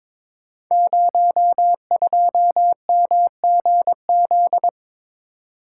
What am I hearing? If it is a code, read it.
Morse code, heard as a keyed tone, about 22 wpm, 700 Hz. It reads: 02MGZ
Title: Code Groups - 5 Letter:Numbers - 22wpm